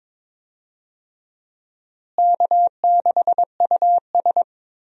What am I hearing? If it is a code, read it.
Morse code, heard as a keyed tone, about 22 wpm, 700 Hz. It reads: K6US
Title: USA Call Signs - 4 Characters - Repeated 2x - 22wpm